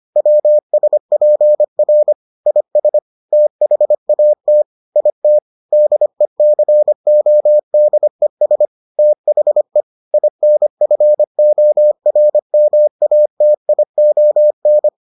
Morse code, heard as a keyed tone, about 25 wpm, 600 Hz. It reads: WSPR IS THAT IT DECODES THE INFORMATION